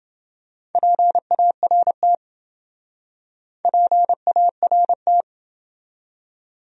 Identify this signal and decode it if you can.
Morse code, heard as a keyed tone, about 28 wpm, 700 Hz. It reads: PART PART